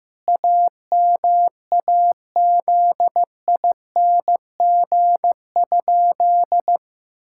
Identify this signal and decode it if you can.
Morse code, heard as a keyed tone, about 15 wpm, 700 Hz. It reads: AMAZING?
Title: Pangram Challenge v2 - 15wpm